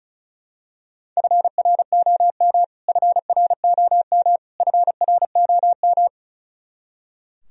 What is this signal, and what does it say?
Morse code, heard as a keyed tone, about 34 wpm, 700 Hz. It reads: FROM FROM FROM
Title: Top 100 Words Repeated 35wpm